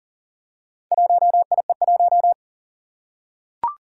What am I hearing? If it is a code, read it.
Morse code, heard as a keyed tone, about 40 wpm, 700 Hz. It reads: 1IE1
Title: Four Challenging Letter-Number Combinations 40wpm